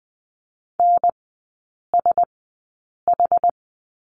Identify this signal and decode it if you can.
Morse code, heard as a keyed tone, about 20 wpm, spaced as for 8 wpm, 700 Hz. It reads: NSH
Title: 074 Lesson 13e - Letter L - Three Characters 20wpm